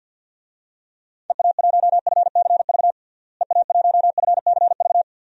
Morse code, heard as a keyed tone, about 52 wpm, 700 Hz. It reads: EA1FCV EA1FCV
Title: DXCC-100 Call Signs - 6 Characters - Repeated 2x - 50wpm